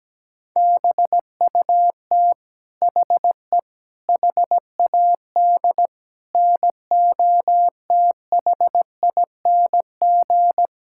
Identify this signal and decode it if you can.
Morse code, heard as a keyed tone, about 17 wpm, 700 Hz. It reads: BUT HE HAD NOTHING
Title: Sentences from Top 1600 Words 17wpm